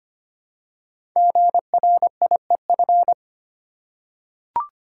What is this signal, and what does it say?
Morse code, heard as a keyed tone, about 25 wpm, 700 Hz. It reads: GRIEF
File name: 5 Letter Words - First 3 Letters Given - 25wpm